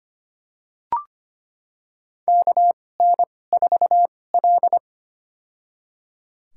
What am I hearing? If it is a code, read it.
Morse code, heard as a keyed tone, about 25 wpm, spaced as for 18 wpm, 700 Hz. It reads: KN4L
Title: USA Call Signs - 4 Characters - 2x Character Spacing - 25wpm